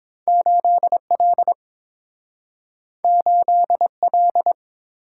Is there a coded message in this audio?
Yes: Morse code from 0.3 to 4.5 s, heard as a keyed tone, about 30 wpm, 700 Hz. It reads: 8L 8L